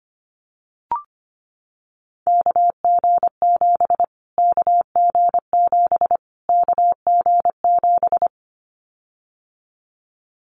Morse code, heard as a keyed tone, about 25 wpm, 700 Hz. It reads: KG7 KG7 KG7
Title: Callsign Prefixes for Top 25 DXCC - Repeated 3x - 25wpm